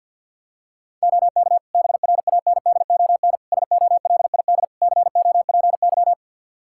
Morse code, heard as a keyed tone, about 47 wpm, 700 Hz. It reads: OK BRANDON SOLID COPY
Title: K1USN SST QSO Practice 50wpm